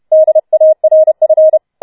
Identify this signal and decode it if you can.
Morse code, heard as a keyed tone, about 31 wpm, 630 Hz. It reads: DARF